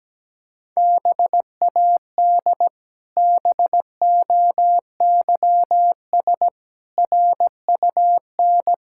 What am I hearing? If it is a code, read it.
Morse code, heard as a keyed tone, about 17 wpm, 700 Hz. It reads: BAD BOYS RUN
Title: Pangram Challenge v2 - 17wpm